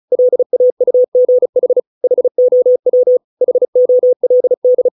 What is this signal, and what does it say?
Morse code, heard as a keyed tone, about 34 wpm, 500 Hz. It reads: LAUGH HOW HOLD